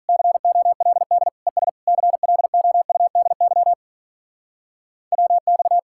Morse code, heard as a keyed tone, about 44 wpm, 700 Hz. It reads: COLD ES CLOUDY WX